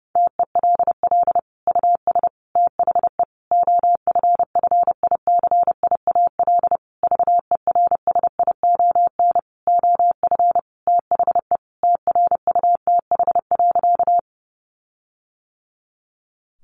Morse code, heard as a keyed tone, about 30 wpm, 700 Hz. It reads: TELL US THE OFFICIAL VERSION OF THE TRUTH.
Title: Sentences from Top 1400 Words 30wpm